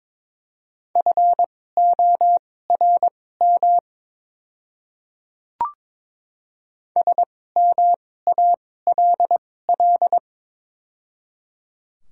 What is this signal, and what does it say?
Morse code, heard as a keyed tone, about 22 wpm, spaced as for 16 wpm, 700 Hz. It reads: FORM SMALL
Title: 2x Character Spacing - Top 300 Words - 22wpm